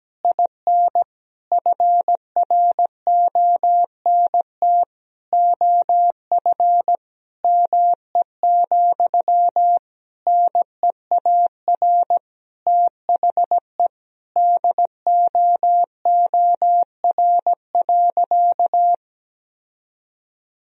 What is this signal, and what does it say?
Morse code, heard as a keyed tone, about 17 wpm, 700 Hz. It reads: IN FRONT OF ME, NEAR THE DOOR.
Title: Sentences from Top 1000 Words - Review Part 2 - 17wpm